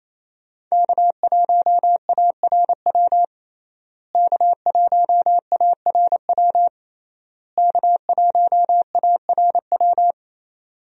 Morse code, heard as a keyed tone, about 28 wpm, 700 Hz. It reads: K1ARW K1ARW K1ARW
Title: USA Call Signs - 5 Characters - Repeated 3x - 28wpm